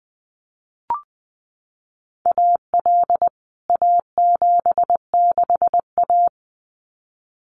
Morse code, heard as a keyed tone, about 20 wpm, 700 Hz. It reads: AL A76A
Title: CWT Exchange 20wpm